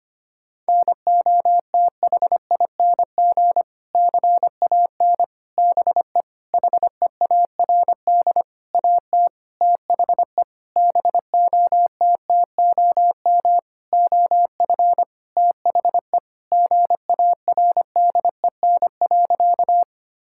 Morse code, heard as a keyed tone, about 25 wpm, 700 Hz. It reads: NOTHING CAN BE HEARD AT THE BOTTOM OF THE GARDEN.
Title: Sentences from Top 1000 Words - Review Part 6 - 25wpm